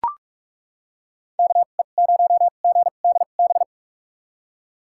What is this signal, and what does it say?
Morse code, heard as a keyed tone, about 43 wpm, spaced as for 32 wpm, 700 Hz. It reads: KE0GDB
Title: USA Call Signs - 6 Characters - 2x Character Spacing - 45wpm